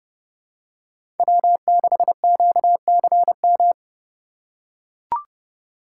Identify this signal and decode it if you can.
Morse code, heard as a keyed tone, about 30 wpm, 700 Hz. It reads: W6QCM
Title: USA Call Signs - 5 Characters - 2x Character Spacing - 30wpm